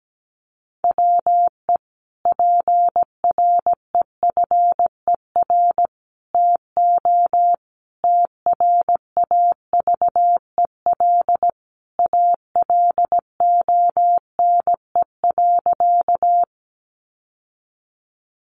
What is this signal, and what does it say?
Morse code, heard as a keyed tone, about 17 wpm, 700 Hz. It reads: WE PREFER TO TRAVEL ALONE.
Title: Sentences from Top 1100 Words 17wpm